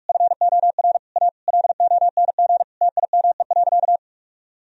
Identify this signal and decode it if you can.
Morse code, heard as a keyed tone, about 41 wpm, 700 Hz. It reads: FOR A LONG TIME.
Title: Sentences from Top 1000 Words - Review Part 4 - 45wpm